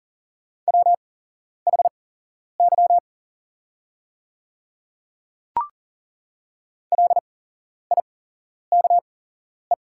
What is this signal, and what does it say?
Morse code, heard as a keyed tone, about 40 wpm, spaced as for 11 wpm, 700 Hz. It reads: WHY LIKE
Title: 8x Character Spacing - Top 200 Words - 40wpm